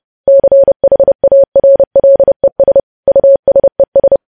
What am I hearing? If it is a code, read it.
Morse code, heard as a keyed tone, about 30 wpm, 550 Hz. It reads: CHARLES USES
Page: news headlines in morse code at 30 wpm